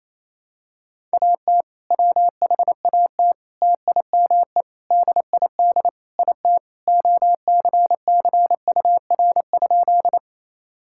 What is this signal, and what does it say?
Morse code, heard as a keyed tone, about 28 wpm, 700 Hz. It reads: AT WHAT TIME DID IT OCCUR?
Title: Sentences from Top 1000 Words - Review Part 6 - 28wpm